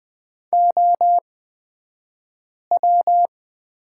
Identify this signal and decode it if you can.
Morse code, heard as a keyed tone, about 20 wpm, spaced as for 5 wpm, 700 Hz. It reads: OW
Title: 249 Lesson 40d - BK - Two Characters 20wpm